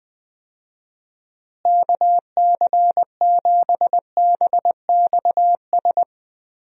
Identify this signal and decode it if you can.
Morse code, heard as a keyed tone, about 20 wpm, 700 Hz. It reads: KC7BXS
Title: USA Call Signs - 6 Characters - Repeated 3x - 20wpm